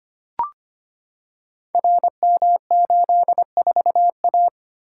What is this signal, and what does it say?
Morse code, heard as a keyed tone, about 25 wpm, 700 Hz. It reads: RM84A
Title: Code Groups - 5 Letter:Numbers - 25wpm